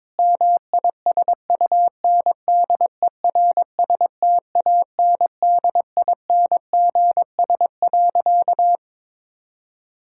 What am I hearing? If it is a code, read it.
Morse code, heard as a keyed tone, about 22 wpm, 700 Hz. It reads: MISUNDERSTANDINGS.
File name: Difficult Sentences with Hints - 22wpm